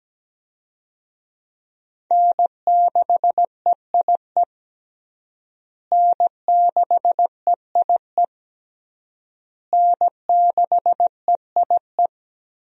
Morse code, heard as a keyed tone, about 17 wpm, 700 Hz. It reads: N6EIE N6EIE N6EIE
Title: USA Call Signs - 5 Characters - Repeated 3x - 17wpm